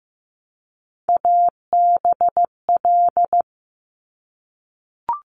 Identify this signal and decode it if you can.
Morse code, heard as a keyed tone, about 15 wpm, 700 Hz. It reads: ABL
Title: Top Three Letter Combinations in Words 15wpm